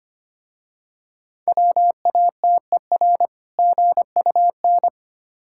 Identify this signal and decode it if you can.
Morse code, heard as a keyed tone, about 25 wpm, 700 Hz. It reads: WATER GUN
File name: practice / 25wpm / Difficult Words with Hints - 25wpm